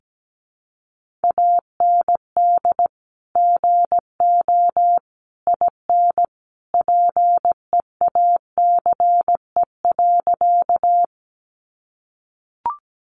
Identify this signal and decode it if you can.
Morse code, heard as a keyed tone, about 17 wpm, 700 Hz. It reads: AND GO IN PEACE.
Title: Sentences from Top 1500 Words 17wpm